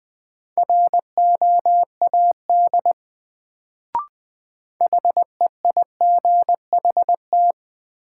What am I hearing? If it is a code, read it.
Morse code, heard as a keyed tone, about 20 wpm, 700 Hz. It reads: ROAD HEIGHT
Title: Sets of 5 Words - New Word Drill - 20wpm